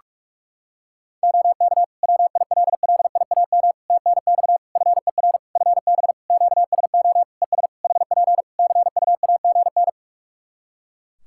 Morse code, heard as a keyed tone, about 46 wpm, 700 Hz. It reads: OK WILLIAM TNX FER FB QSO ES HP CUAGN
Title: Practice QSO - Exchange 4 - Ending - 45wpm